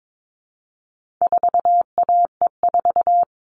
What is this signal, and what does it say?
Morse code, heard as a keyed tone, about 22 wpm, 700 Hz. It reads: 4AE4
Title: All Cut Numbers - 4-Digit 22wpm